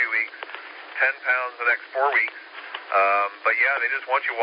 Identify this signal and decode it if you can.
Emergency Alert System (SAME) tones, heard as harsh buzzing bursts, with speech. It is heard from 0.9 to 4.0 s.